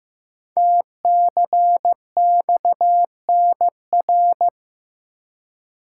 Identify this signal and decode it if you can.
Morse code, heard as a keyed tone, about 15 wpm, 700 Hz. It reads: TCXNR
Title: Code Groups - 5 Letters - 15wpm